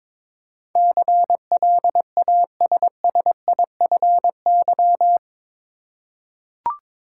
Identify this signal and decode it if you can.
Morse code, heard as a keyed tone, about 22 wpm, 700 Hz. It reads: CLASSIFY
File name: practice / 22wpm / Eight Letter Words - 22wpm